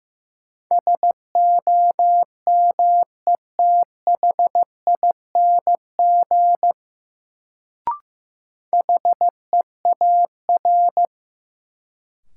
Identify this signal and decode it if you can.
Morse code, heard as a keyed tone, about 15 wpm, 700 Hz. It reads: SOMETHING HEAR
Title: Sets of 5 Words - New Word Drill - 15wpm